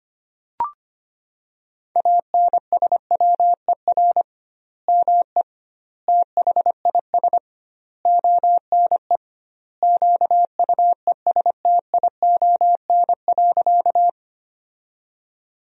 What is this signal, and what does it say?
Morse code, heard as a keyed tone, about 25 wpm, 700 Hz. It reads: ANSWER ME THIS ONE QUESTION.
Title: 2x Word-Spacing - Sentences from Top 400 Words - 25wpm